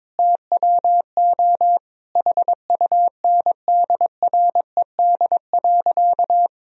Morse code, heard as a keyed tone, about 22 wpm, 700 Hz. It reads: TWO HUNDRED.
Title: Sentences from Top 1000 Words - Review Part 2 - 22wpm